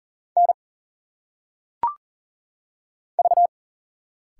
Morse code, heard as a keyed tone, about 40 wpm, 700 Hz. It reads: N V
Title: Single Letters - Phonetic Pronunciation 40wpm